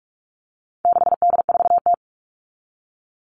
Morse code, heard as a keyed tone, about 44 wpm, 700 Hz. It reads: 6D4T
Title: All Cut Numbers - 4-Digit 45wpm